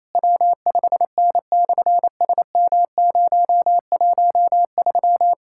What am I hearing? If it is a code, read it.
Morse code, heard as a keyed tone, about 28 wpm, 700 Hz. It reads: W5N/SM013
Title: SOTA QSO Practice 28wpm